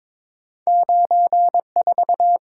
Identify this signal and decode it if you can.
Morse code, heard as a keyed tone, about 22 wpm, 700 Hz. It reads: 94
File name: practice / 22wpm / Two Letter-Number Combinations - 22wpm